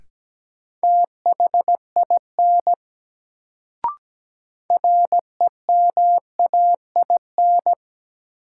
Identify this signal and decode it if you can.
Morse code, heard as a keyed tone, about 17 wpm, 700 Hz. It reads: THIN REMAIN
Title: Sentences from Top 1200 Words - New Word Drill - 17wpm